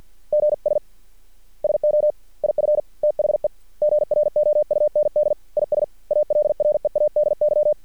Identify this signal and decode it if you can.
Morse code, heard as a keyed tone, about 50 wpm, 600 Hz. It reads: GS SO IF THE GROUND IS ALREADY
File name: archived / CRSnet